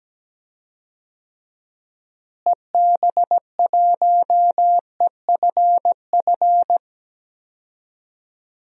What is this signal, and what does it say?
Morse code, heard as a keyed tone, about 17 wpm, 700 Hz. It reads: EB1EFF
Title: DXCC-100 Call Signs - 6 Characters - Repeated 3x - 17wpm